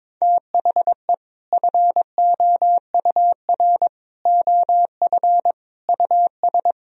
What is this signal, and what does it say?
Morse code, heard as a keyed tone, about 21 wpm, 700 Hz. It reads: THE FOUR OF US
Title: Sentences from Top 1000 Words - Review Part 1 - 22wpm